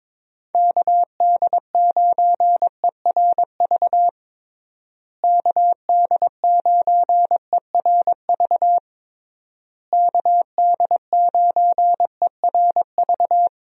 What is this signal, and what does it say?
Morse code, heard as a keyed tone, about 22 wpm, 700 Hz. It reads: KD9ERV KD9ERV KD9ERV
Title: USA Call Signs - 6 Characters - Repeated 3x - 22wpm